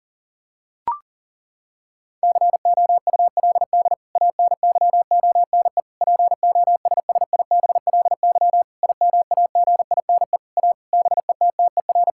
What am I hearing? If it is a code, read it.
Morse code, heard as a keyed tone, about 40 wpm, 700 Hz. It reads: COULD ANYONE POSSIBLY IMAGINE A BETTER
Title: Sentences from Top 1600 Words 40wpm